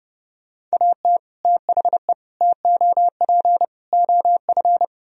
Morse code, heard as a keyed tone, about 38 wpm, 700 Hz. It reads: AT THE TOP OF
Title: Speed-Racing - Sets of 4 Words 30wpm